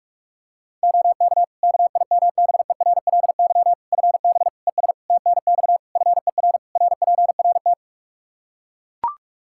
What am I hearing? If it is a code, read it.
Morse code, heard as a keyed tone, about 41 wpm, 700 Hz. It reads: OK KIMBERLY FB ES TNX FER RPRT
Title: Practice QSO - Exchange 2 - Info - 45wpm